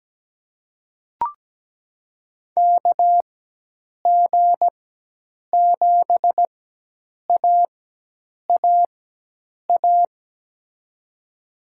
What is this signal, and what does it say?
Morse code, heard as a keyed tone, about 17 wpm, spaced as for 8 wpm, 700 Hz. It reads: KG7AAA
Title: USA Call Signs - 6 Characters - 4x Character Spacing - 17wpm